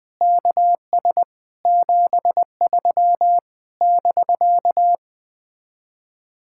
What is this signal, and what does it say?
Morse code, heard as a keyed tone, about 20 wpm, 700 Hz. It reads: KS 73 <BK>